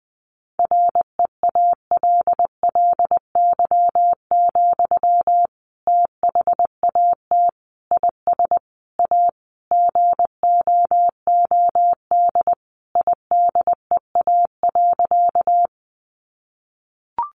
Morse code, heard as a keyed tone, about 20 wpm, 700 Hz. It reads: REALLY, THAT IS A GOOD IDEA.
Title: Sentences from Top 600 Words 20wpm